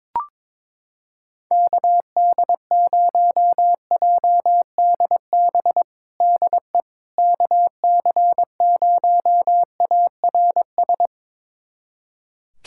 Morse code, heard as a keyed tone, about 22 wpm, 700 Hz. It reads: KD0JDB DE KC0ARS
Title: USA Callsign DE Callsign - 6 Characters - 22wpm